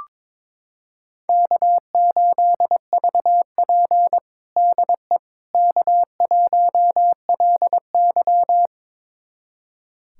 Morse code, heard as a keyed tone, about 22 wpm, 700 Hz. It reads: K8VP DE K1LY
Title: USA Callsign DE Callsign - 4 Characters - 22wpm